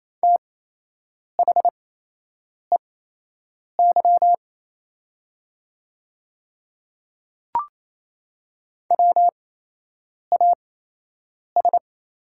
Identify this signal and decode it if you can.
Morse code, heard as a keyed tone, about 28 wpm, spaced as for 8 wpm, 700 Hz. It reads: THEY WAS
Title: 8x Character Spacing - Top 100 Words - 28wpm